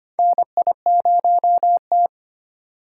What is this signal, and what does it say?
Morse code, heard as a keyed tone, about 25 wpm, 700 Hz. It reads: NI0T
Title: USA Call Signs - 4 Characters - 3x Character Spacing - 25wpm